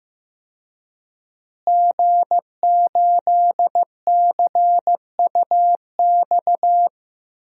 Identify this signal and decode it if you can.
Morse code, heard as a keyed tone, about 15 wpm, 700 Hz. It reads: G8CUX
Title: DXCC-100 Call Signs - 5 Characters - Repeated 2x - 15wpm